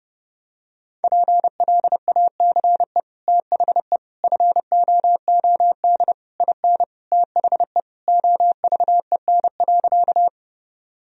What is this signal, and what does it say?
Morse code, heard as a keyed tone, about 30 wpm, 700 Hz. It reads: PLACE THE FOOD IN THE OVEN.
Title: Sentences from Top 1000 Words - Review Part 4 - 30wpm